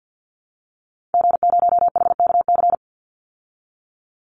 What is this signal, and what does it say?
Morse code, heard as a keyed tone, about 51 wpm, 700 Hz. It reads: G0HKC
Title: Call Signs Repeated 2x - 50wpm